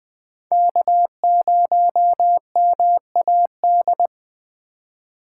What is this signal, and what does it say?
Morse code, heard as a keyed tone, about 21 wpm, 700 Hz. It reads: K0MAD
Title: Speed Racing - USA 1x3 Call Signs 17wpm